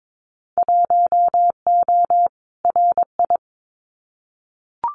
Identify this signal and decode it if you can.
Morse code, heard as a keyed tone, about 22 wpm, 700 Hz. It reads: 1O RI
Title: Winter Field Day Exchanges 22wpm